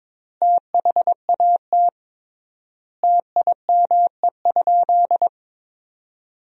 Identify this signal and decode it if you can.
Morse code, heard as a keyed tone, about 21 wpm, 700 Hz. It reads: THAT TIME?
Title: 3x Word-Spacing - Sentences from Top 200 Words - 22wpm